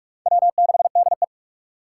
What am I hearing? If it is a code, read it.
Morse code, heard as a keyed tone, about 44 wpm, 700 Hz. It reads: W6DE